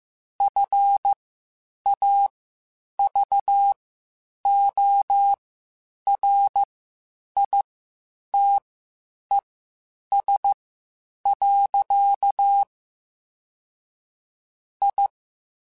Morse code, heard as a keyed tone, about 15 wpm, spaced as for 8 wpm, 800 Hz. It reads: FAVORITES. I
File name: (spaced) / QSOs Spaced-mp3s / WD0FBT 1 @13 sp